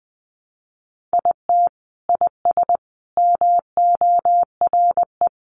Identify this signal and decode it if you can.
Morse code, heard as a keyed tone, about 20 wpm, 700 Hz. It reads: IT IS MORE
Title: Sentences from Top 1100 Words 20wpm